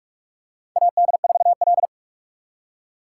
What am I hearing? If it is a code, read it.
Morse code, heard as a keyed tone, about 44 wpm, 700 Hz. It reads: AD4L
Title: USA Call Signs - 4 Characters - 6x Character Spacing - 45wpm